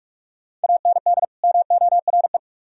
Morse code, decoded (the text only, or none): AND MORE